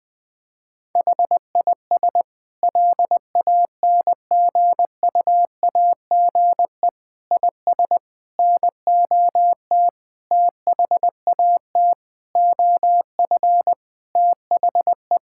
Morse code, heard as a keyed tone, about 20 wpm, 700 Hz. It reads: HIS LANGUAGE IS NOT THAT OF THE
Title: Sentences from Top 1600 Words 20wpm